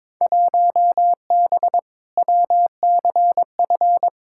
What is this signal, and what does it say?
Morse code, heard as a keyed tone, about 22 wpm, 700 Hz. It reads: 1B WCF
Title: Field Day Exchanges Encore - 22wpm